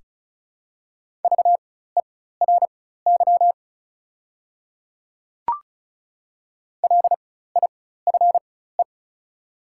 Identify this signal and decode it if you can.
Morse code, heard as a keyed tone, about 36 wpm, spaced as for 16 wpm, 690 Hz. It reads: VERY LIFE
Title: 4x Character Spacing - Top 300 Words - 35wpm